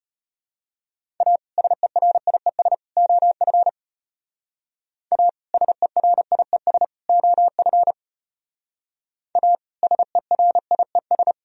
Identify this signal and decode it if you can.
Morse code, heard as a keyed tone, about 33 wpm, 700 Hz. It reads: A SERIES OF A SERIES OF A SERIES